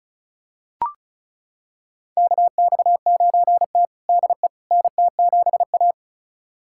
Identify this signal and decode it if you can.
Morse code, heard as a keyed tone, about 36 wpm, 700 Hz. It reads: KX9T DE NT7A
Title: USA Callsign DE Callsign - 4 Characters - 35wpm